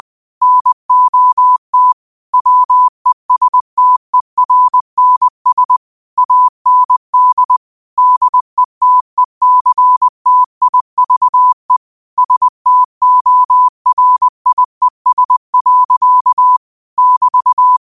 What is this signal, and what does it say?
Morse code, heard as a keyed tone, about 20 wpm, 1000 Hz. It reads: NOT WESTERNS AND DETECTIVE STORIES. <BT>
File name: qotdfiles / 20wpm